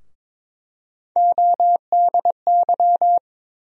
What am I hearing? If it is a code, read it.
Morse code, heard as a keyed tone, about 22 wpm, 700 Hz. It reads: ODY
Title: USA Call Sign Postfixes - 3 Letters 22wpm